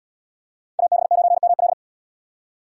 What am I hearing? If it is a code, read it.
Morse code, heard as a keyed tone, about 77 wpm, 700 Hz. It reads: AB9ML